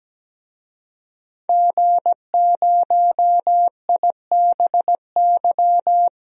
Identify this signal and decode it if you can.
Morse code, heard as a keyed tone, about 17 wpm, 700 Hz. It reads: G0IBY
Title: DXCC-100 Call Signs - 5 Characters - Repeated 2x - 17wpm